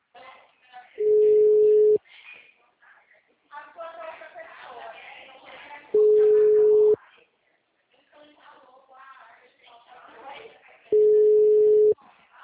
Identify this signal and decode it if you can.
Ringback tone (Europe, 425 Hz), with speech.